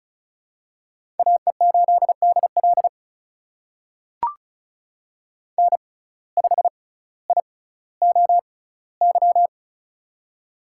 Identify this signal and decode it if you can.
Morse code, heard as a keyed tone, about 36 wpm, spaced as for 18 wpm, 700 Hz. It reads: AE8DL N5IOY
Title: USA Call Signs - 5 Characters - 6x Character Spacing - 35wpm